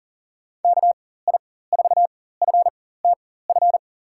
Morse code, heard as a keyed tone, about 40 wpm, spaced as for 19 wpm, 700 Hz. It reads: KI4FTF
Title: USA Call Signs - 6 Characters - 4x Character Spacing - 40wpm